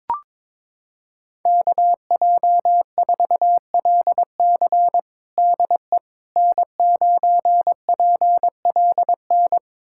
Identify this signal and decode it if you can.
Morse code, heard as a keyed tone, about 22 wpm, 700 Hz. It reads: KJ4LC DE N9PLN
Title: USA Callsign DE Callsign - 5 Characters - 22wpm